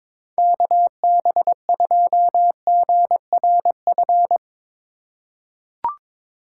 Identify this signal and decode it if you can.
Morse code, heard as a keyed tone, about 22 wpm, 700 Hz. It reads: KB2GRF
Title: USA Call Signs - 6 Characters - 4x Character Spacing - 22wpm